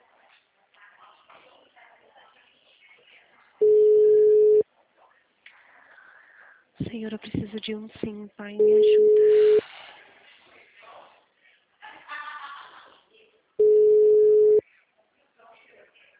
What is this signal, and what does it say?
Ringback tone (Europe, 425 Hz), with speech.